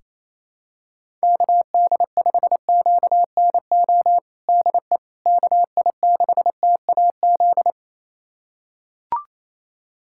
Morse code, heard as a keyed tone, about 28 wpm, 700 Hz. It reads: KD5QNO DE KI6TAZ
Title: USA Callsign DE Callsign 28wpm